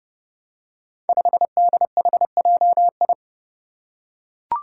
Morse code, heard as a keyed tone, about 31 wpm, 700 Hz. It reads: 5DHJI
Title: Five Challenging Letter-Number Combinations 30wpm